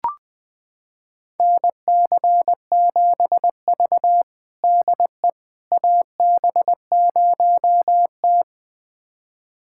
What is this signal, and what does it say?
Morse code, heard as a keyed tone, about 20 wpm, 700 Hz. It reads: NC7V DE AB0T
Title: USA Callsign DE Callsign - 4 Characters - 20wpm